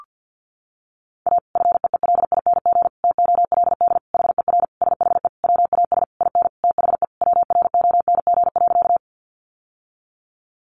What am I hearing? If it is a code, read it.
Morse code, heard as a keyed tone, about 50 wpm, 700 Hz. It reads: A FEELING TOLD HER SHE WAS IN THE WRONG.
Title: Sentences from Top 1200 Words 50wpm